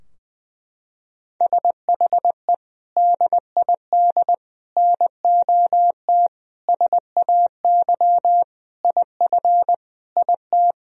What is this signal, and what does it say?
Morse code, heard as a keyed tone, about 20 wpm, 700 Hz. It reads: SHE DID NOT SAY IF IT
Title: Sentences from Top 1000 Words - Review Part 3 - 20wpm